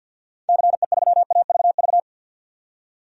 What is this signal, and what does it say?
Morse code, heard as a keyed tone, about 53 wpm, 700 Hz. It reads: CE3AVV
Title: DXCC-100 Call Signs - 6 Characters - 50wpm